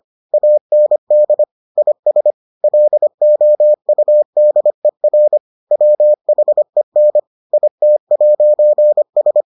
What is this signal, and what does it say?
Morse code, heard as a keyed tone, about 25 wpm, 600 Hz. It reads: AND IS LOUDER WHEN IT'S